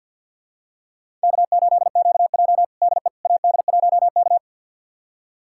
Morse code, heard as a keyed tone, about 52 wpm, 700 Hz. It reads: K8QJ DE AD1K
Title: USA Callsign DE Callsign - 4 Characters - 50wpm